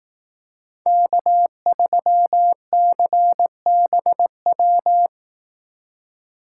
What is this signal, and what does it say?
Morse code, heard as a keyed tone, about 18 wpm, 700 Hz. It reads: K3CBW
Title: Speed Racing - USA 1x3 Call Signs 15wpm